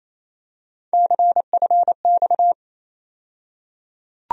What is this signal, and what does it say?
Morse code, heard as a keyed tone, about 28 wpm, 700 Hz. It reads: CFX
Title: USA Call Sign Postfixes - 3 Letters 28wpm